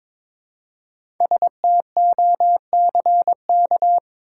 Morse code, heard as a keyed tone, about 22 wpm, 700 Hz. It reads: STOCK
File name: 5 Letter Words - First 3 Letters Given - 22wpm